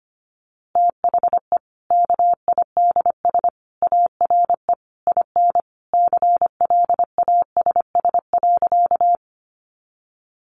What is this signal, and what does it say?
Morse code, heard as a keyed tone, about 25 wpm, 700 Hz. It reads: THE KIDS ARE IN CLASS.
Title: Easy Phrases 25wpm